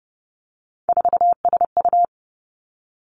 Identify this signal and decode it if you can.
Morse code, heard as a keyed tone, about 32 wpm, 700 Hz. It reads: 4SU
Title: Three Challenging Letter-Number Combinations 30wpm